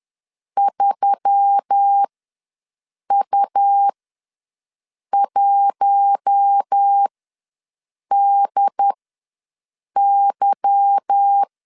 Morse code, heard as a keyed tone, about 10 wpm, spaced as for 6 wpm, 800 Hz. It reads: 3U1DY
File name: LEZIONI / 10 SIGNS / 10B INT ws 10wpm 4134